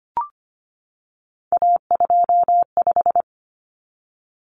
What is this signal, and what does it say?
Morse code, heard as a keyed tone, about 25 wpm, 700 Hz. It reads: A25
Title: Common Cut Numbers - 3-Digit 25wpm